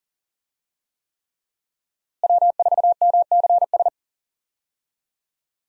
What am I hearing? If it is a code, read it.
Morse code, heard as a keyed tone, about 40 wpm, 700 Hz. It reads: W4MCS
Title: USA Call Signs - 5 Characters - Repeated 3x - 40wpm